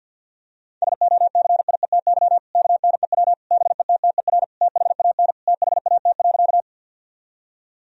Morse code, heard as a keyed tone, about 50 wpm, 700 Hz. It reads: SOCIETY KNEW BETTER THAN THAT.
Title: Sentences from Top 1000 Words - Review Part 4 - 50wpm